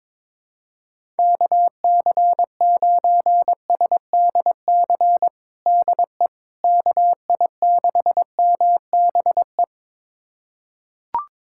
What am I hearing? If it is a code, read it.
Morse code, heard as a keyed tone, about 21 wpm, 700 Hz. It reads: KC9SDC DE KI6MBE
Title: USA Callsign DE Callsign - 6 Characters - 22wpm